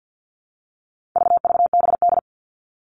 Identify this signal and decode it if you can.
Morse code, heard as a keyed tone, about 50 wpm, 700 Hz. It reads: VVBD
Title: All Cut Numbers - 4-Digit 50wpm